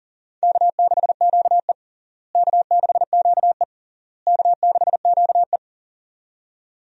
Morse code, heard as a keyed tone, about 40 wpm, 700 Hz. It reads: K6QE K6QE K6QE
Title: USA Call Signs - 4 Characters - Repeated 3x - 40wpm